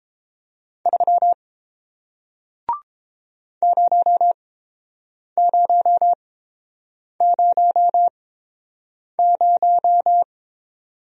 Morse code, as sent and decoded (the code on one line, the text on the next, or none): ...-- / ----- ----- ----- -----
3 0000